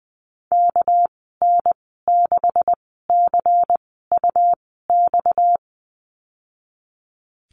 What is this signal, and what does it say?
Morse code, heard as a keyed tone, about 20 wpm, spaced as for 15 wpm, 700 Hz. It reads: KN6CUX
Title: 231 Lesson 37g - Letter x - Call Signs 20wpm